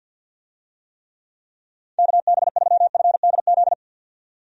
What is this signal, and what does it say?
Morse code, heard as a keyed tone, about 45 wpm, 700 Hz. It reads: KB3FDZ
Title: USA Call Signs - 6 Characters - 50wpm